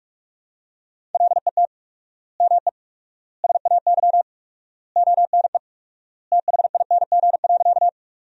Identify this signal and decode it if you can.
Morse code, heard as a keyed tone, about 41 wpm, 700 Hz. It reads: LET ME SAY ONE THING.
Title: 4x Word-Spacing - Sentences from Top 300 Words - 45wpm